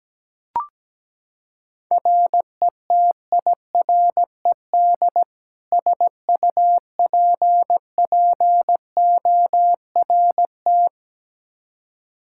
Morse code, heard as a keyed tone, about 17 wpm, 700 Hz. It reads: RETIRED SUPPORT
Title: K1USN SST QSO Practice 17wpm